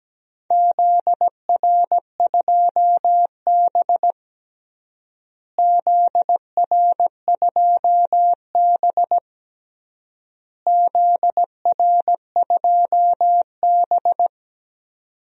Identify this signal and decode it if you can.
Morse code, heard as a keyed tone, about 17 wpm, 700 Hz. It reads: ZR2B ZR2B ZR2B
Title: DXCC-100 Call Signs - 4 Characters - Repeated 3x - 17wpm